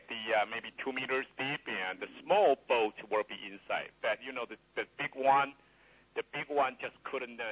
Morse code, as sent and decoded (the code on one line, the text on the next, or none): none